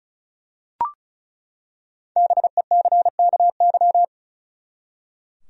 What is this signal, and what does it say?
Morse code, heard as a keyed tone, about 36 wpm, 700 Hz. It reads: BECKY
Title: USA Common Names - Max 5 Letters - 35wpm